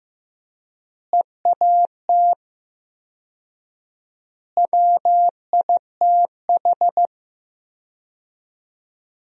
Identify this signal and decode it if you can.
Morse code, heard as a keyed tone, about 15 wpm, 700 Hz. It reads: EAT WITH
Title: 4x Word-Spacing - Sentences from Top 300 Words - 15wpm